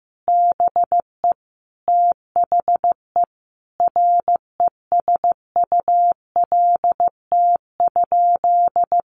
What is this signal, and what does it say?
Morse code, heard as a keyed tone, about 15 wpm, 700 Hz. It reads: BE THE RESULT?
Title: Sentences from Top 700 Words 15wpm